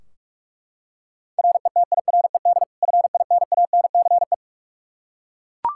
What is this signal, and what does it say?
Morse code, heard as a keyed tone, about 42 wpm, 700 Hz. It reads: RETIRED FINANCE
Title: K1USN SST QSO Practice 45wpm